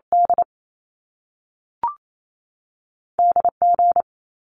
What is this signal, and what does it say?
Morse code, heard as a keyed tone, about 28 wpm, 700 Hz. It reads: D DG
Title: Top Two Letter Combinations in Words 28wpm